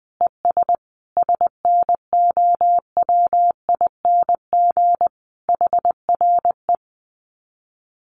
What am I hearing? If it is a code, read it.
Morse code, heard as a keyed tone, about 20 wpm, 700 Hz. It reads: ES SNOWING HRE
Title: Common QSO Phrases 20wpm